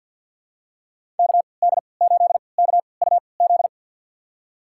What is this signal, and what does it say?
Morse code, heard as a keyed tone, about 53 wpm, spaced as for 29 wpm, 700 Hz. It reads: KD8KUZ
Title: USA Call Signs - 6 Characters - 3x Character Spacing - 50wpm